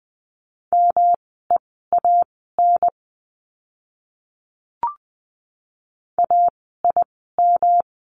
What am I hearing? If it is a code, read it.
Morse code, heard as a keyed tone, about 20 wpm, spaced as for 15 wpm, 700 Hz. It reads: MEAN AIM
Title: 108 Lesson 18f - Letter m - Words 20wpm